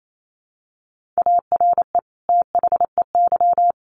Morse code, heard as a keyed tone, about 28 wpm, 700 Hz. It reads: ARE THEY
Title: Sentences from Top 1400 Words 28wpm